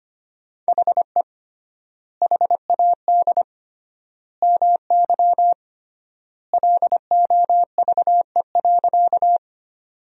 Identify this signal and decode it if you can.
Morse code, heard as a keyed tone, about 25 wpm, 700 Hz. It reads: HE HAD MY LOVE.
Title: 3x Word-Spacing - Sentences from Top 400 Words - 25wpm